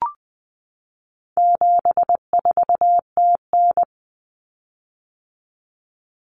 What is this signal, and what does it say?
Morse code, heard as a keyed tone, about 20 wpm, 700 Hz. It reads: 74TN
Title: Common Cut Numbers - 4-Digit 20wpm